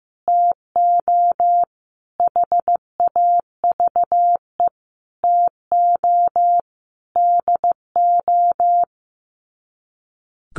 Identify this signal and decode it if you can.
Morse code, heard as a keyed tone, about 15 wpm, 700 Hz. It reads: TO HAVE TO DO
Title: Sets of 5 Words Encore 15wpm